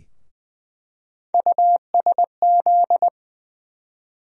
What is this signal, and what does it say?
Morse code, heard as a keyed tone, about 20 wpm, 700 Hz. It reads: USZ
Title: USA Call Sign Postfixes - 3 Letters 20wpm